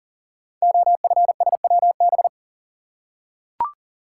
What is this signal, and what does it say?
Morse code, heard as a keyed tone, about 39 wpm, 700 Hz. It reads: OFSWB
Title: Code Groups - 5 Letters - 40wpm